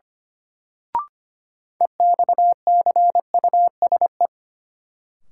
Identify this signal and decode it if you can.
Morse code, heard as a keyed tone, about 25 wpm, 700 Hz. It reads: EXCUSE
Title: Sentences from Top 700 Words - New Word Drill - 25wpm